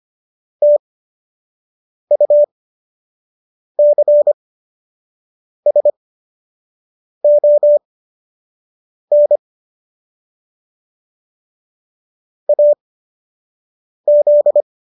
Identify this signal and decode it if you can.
Morse code, heard as a keyed tone, about 25 wpm, spaced as for 6 wpm, 590 Hz. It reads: TUCSON AZ